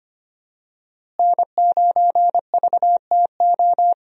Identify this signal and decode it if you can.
Morse code, heard as a keyed tone, about 25 wpm, 700 Hz. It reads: N9VTO
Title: USA Call Signs - 5 Characters - Repeated 3x - 25wpm